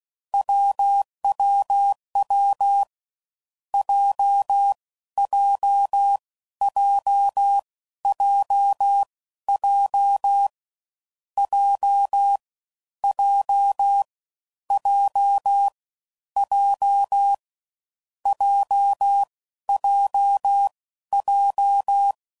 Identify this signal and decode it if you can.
Morse code, heard as a keyed tone, about 16 wpm, spaced as for 12 wpm, 800 Hz. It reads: WWW JJJJJ J J J J JJJ
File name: Combinations / Difficult Combos / AWJ1